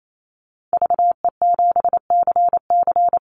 Morse code, heard as a keyed tone, about 28 wpm, 700 Hz. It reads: VE7CC
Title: Call Signs - 28wpm